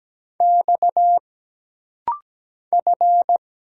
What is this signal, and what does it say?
Morse code, heard as a keyed tone, about 17 wpm, 700 Hz. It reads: X F
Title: Single Letter-Number Alt - Rapid-Fire 17wpm